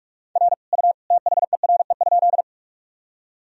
Morse code, heard as a keyed tone, about 46 wpm, 700 Hz. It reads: R U THERE?
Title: Sentences from Top 100 Words and Abbreviations 45wpm